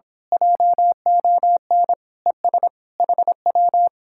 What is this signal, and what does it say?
Morse code, heard as a keyed tone, about 26 wpm, 700 Hz. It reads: JON ES HW
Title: Speed-Racing Common QSO Phrases - 22wpm